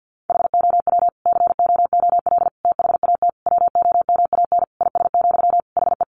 Morse code, heard as a keyed tone, about 50 wpm, 700 Hz. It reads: HOW COOL THAT WOMAN IS, HE
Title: Sentences from Top 1200 Words 50wpm